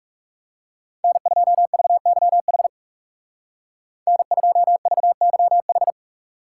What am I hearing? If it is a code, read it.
Morse code, heard as a keyed tone, about 36 wpm, 700 Hz. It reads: N2VYH N2VYH
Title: Speed Racing - USA 1x3 Call Signs 30wpm